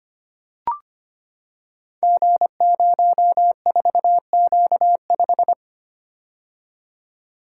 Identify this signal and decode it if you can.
Morse code, heard as a keyed tone, about 25 wpm, 700 Hz. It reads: G04Q5
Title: Code Groups - 5 Letter:Numbers - 25wpm